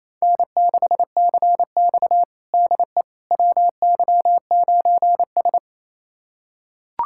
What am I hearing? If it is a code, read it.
Morse code, heard as a keyed tone, about 28 wpm, 700 Hz. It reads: N6CX DE WY9S
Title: USA Callsign DE Callsign - 4 Characters - 28wpm